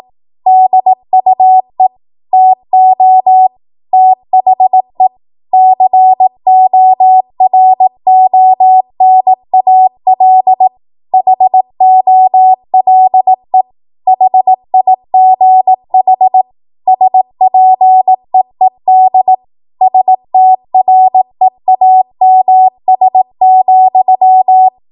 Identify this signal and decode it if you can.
Morse code, heard as a keyed tone, about 18 wpm, 750 Hz. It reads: DUE TO THE CORONAL HOLE HIGH SPEED STREAMS,
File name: Morse / 18WPM